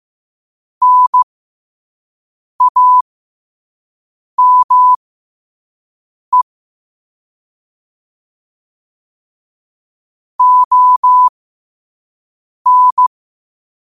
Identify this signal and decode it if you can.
Morse code, heard as a keyed tone, about 15 wpm, spaced as for 5 wpm, 1000 Hz. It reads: NAME ON
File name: qotdfiles / 5wpm